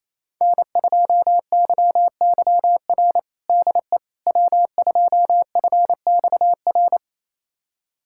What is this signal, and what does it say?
Morse code, heard as a keyed tone, about 28 wpm, 700 Hz. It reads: N2YYR DE W2FXR
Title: USA Callsign DE Callsign - 5 Characters - 28wpm